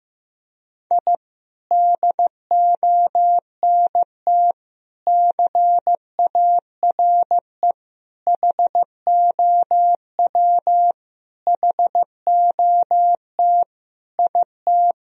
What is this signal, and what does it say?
Morse code, heard as a keyed tone, about 15 wpm, 700 Hz. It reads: I DONT CARE HOW HOT IT
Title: Sentences from Top 1000 Words - Review Part 3 - 15wpm